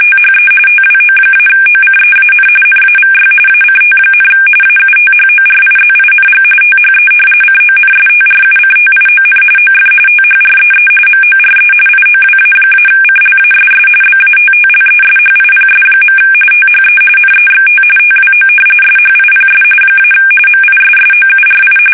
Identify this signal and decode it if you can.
RTTY, about 45.45 baud. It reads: THE TWO-OCEAN NAVY ACT, AUTHORIZING THE CONSTRUCTION OF THREE MORE IOWA-CLASS BATTLESHIPS. THE NEXT TWO SHIPS - MISSOURI
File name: fsk850 / battleshipnj